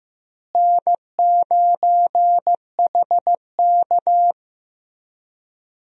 Morse code, heard as a keyed tone, about 16 wpm, 700 Hz. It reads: N9HK
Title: USA Call Signs - 4 Characters - Repeated 3x - 15wpm